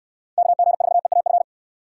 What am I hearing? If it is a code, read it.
Morse code, heard as a keyed tone, about 63 wpm, 700 Hz. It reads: KG4EAF